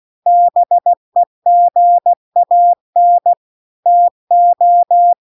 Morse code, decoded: BEGAN TO

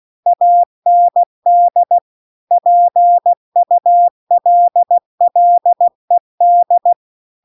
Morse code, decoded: AND PULLED

